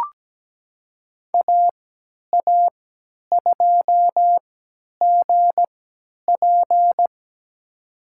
Morse code, heard as a keyed tone, about 17 wpm, spaced as for 10 wpm, 700 Hz. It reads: AA2GP
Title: USA Call Signs - 5 Characters - 3x Character Spacing - 17wpm